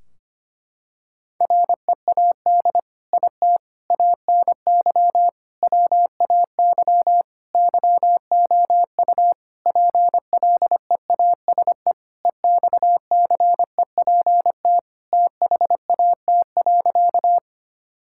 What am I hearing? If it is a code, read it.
Morse code, heard as a keyed tone, about 25 wpm, 700 Hz. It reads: READ IT ANY WAY YOU PLEASE EXCEPT THAT.
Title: Sentences from Top 1600 Words 25wpm